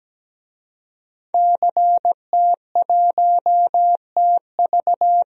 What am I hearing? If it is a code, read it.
Morse code, heard as a keyed tone, about 17 wpm, 700 Hz. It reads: CT1TV
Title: DXCC-100 Call Signs - 5 Characters - 17wpm